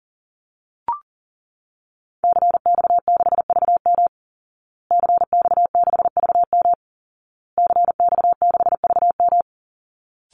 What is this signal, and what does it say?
Morse code, heard as a keyed tone, about 40 wpm, 700 Hz. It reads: CX6VM CX6VM CX6VM
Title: Call Signs Repeated 2x - 40wpm